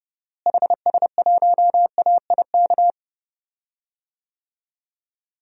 Morse code, heard as a keyed tone, about 30 wpm, 700 Hz. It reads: HS1AIK
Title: DXCC-100 Call Signs - 6 Characters - Repeated 2x - 30wpm